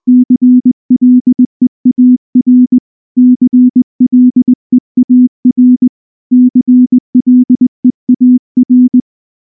Morse code, heard as a keyed tone, about 19 wpm, 250 Hz. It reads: CLEAR CLEAR CLEAR
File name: morse / morse-training-20WPM-10